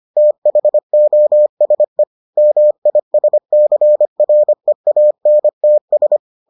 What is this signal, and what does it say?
Morse code, heard as a keyed tone, about 25 wpm, 600 Hz. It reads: THOSE MISCREANTS